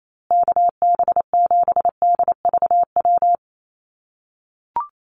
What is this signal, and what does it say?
Morse code, heard as a keyed tone, about 28 wpm, 700 Hz. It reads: KB7DVW
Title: USA Call Signs Repeated 3x 28wpm